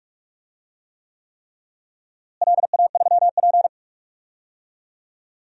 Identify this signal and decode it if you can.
Morse code, heard as a keyed tone, about 47 wpm, 700 Hz. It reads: LA3P
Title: DXCC-100 Call Signs - 4 Characters - Repeated 3x - 45wpm